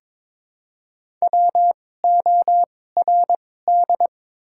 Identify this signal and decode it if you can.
Morse code, heard as a keyed tone, about 22 wpm, spaced as for 16 wpm, 700 Hz. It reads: WORD
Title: 2x Character Spacing - Top 200 Words - 22wpm